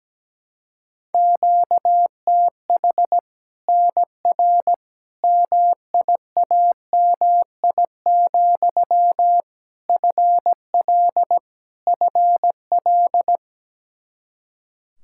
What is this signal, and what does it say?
Morse code, heard as a keyed tone, about 17 wpm, 700 Hz. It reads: QTH NR MIAMI, FL FL
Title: Practice QSO - Exchange 1 - Beginning - 17wpm